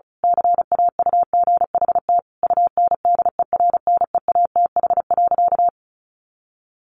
Morse code, heard as a keyed tone, about 34 wpm, 700 Hz. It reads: CAUGHT UNDERNEATH.
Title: A Sound Of Thunder 35wpm